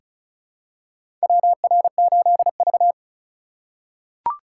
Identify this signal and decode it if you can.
Morse code, heard as a keyed tone, about 34 wpm, 700 Hz. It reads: WR8V
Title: USA Call Signs - 4 Characters - 2x Character Spacing - 35wpm